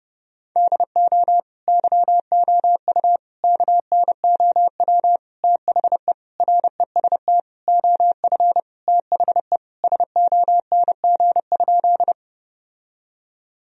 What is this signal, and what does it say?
Morse code, heard as a keyed tone, about 29 wpm, 700 Hz. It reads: DO YOU KNOW THE REST OF THE SONG?
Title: Sentences from Top 1000 Words - Review Part 2 - 30wpm